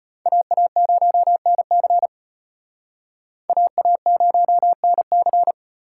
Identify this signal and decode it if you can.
Morse code, heard as a keyed tone, about 34 wpm, 700 Hz. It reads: AA0NC AA0NC